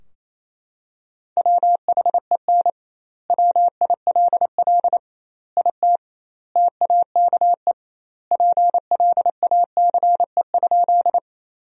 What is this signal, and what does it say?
Morse code, heard as a keyed tone, about 27 wpm, 700 Hz. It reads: WHEN WILL IT TAKE PLACE?
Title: 2x Word-Spacing - Sentences from Top 200 Words - 28wpm